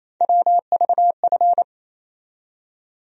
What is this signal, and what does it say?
Morse code, heard as a keyed tone, about 28 wpm, 700 Hz. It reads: WVF
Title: USA Call Sign Postfixes - 3 Letters 28wpm